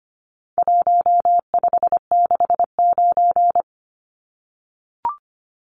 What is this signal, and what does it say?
Morse code, heard as a keyed tone, about 25 wpm, 700 Hz. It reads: 1569
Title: Numbers - Contest Style - 4-Digit 25wpm